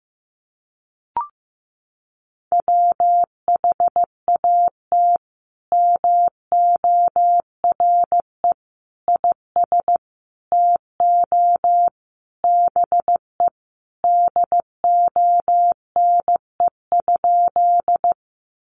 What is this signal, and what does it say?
Morse code, heard as a keyed tone, about 15 wpm, 700 Hz. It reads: WHAT MORE IS TO BE DONE?
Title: Sentences from Top 500 Words 15wpm